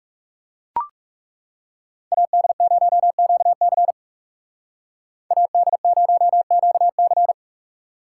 Morse code, heard as a keyed tone, about 36 wpm, 700 Hz. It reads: AD0QC AD0QC
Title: Speed Racing - USA 2x2 Call Signs 30wpm